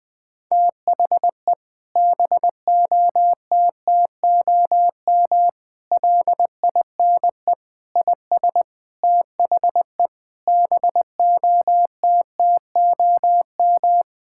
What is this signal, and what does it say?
Morse code, heard as a keyed tone, about 20 wpm, 700 Hz. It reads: THE BOTTOM LINE IS THE BOTTOM
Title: Sentences from Top 1000 Words - Review Part 6 - 20wpm